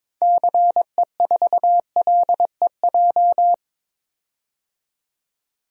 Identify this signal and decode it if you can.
Morse code, heard as a keyed tone, about 22 wpm, 700 Hz. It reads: CE4LEJ
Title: DXCC-100 Call Signs - 6 Characters - 22wpm